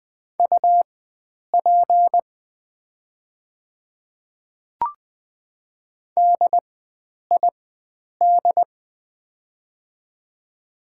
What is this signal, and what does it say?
Morse code, heard as a keyed tone, about 20 wpm, spaced as for 9 wpm, 700 Hz. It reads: UP DID
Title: 4x Character Spacing - Top 100 Words - 20wpm